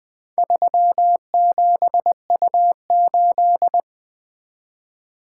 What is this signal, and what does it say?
Morse code, heard as a keyed tone, about 20 wpm, 700 Hz. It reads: 37U8
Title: Four Letter-Number Combinations - 20wpm